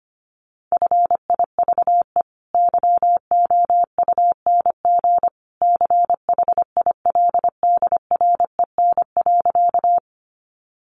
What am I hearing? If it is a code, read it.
Morse code, heard as a keyed tone, about 25 wpm, 700 Hz. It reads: FIVE YOUNG CHILDREN.
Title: Sentences from Top 400 Words 25wpm